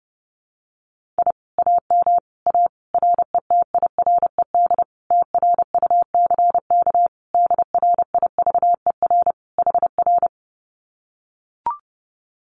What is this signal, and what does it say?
Morse code, heard as a keyed tone, about 30 wpm, 700 Hz. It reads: I AM A RETIRED TRUCK DRIVER HR
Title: Common QSO Phrases Encore 30wpm